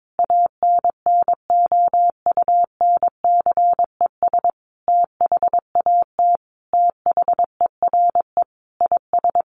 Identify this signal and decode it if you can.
Morse code, heard as a keyed tone, about 22 wpm, 700 Hz. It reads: ANNOUNCES THAT THERE IS
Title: Additional Easy Phrases 22wpm